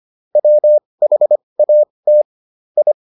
Morse code, heard as a keyed tone, about 25 wpm, spaced as for 20 wpm, 600 Hz. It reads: WHAT I